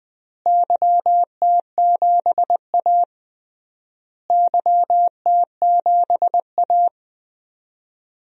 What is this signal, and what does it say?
Morse code, heard as a keyed tone, about 21 wpm, 690 Hz. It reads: YT7A YT7A
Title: DXCC-100 Call Signs - 4 Characters - Repeated 3x - 20wpm